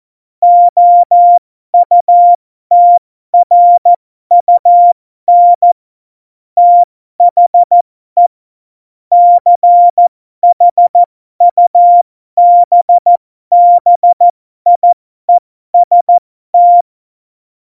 Morse code, heard as a keyed tone, about 14 wpm, spaced as for 12 wpm, 700 Hz. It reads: OUTRUN THE CHUBBIEST
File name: Morsekurs / Witze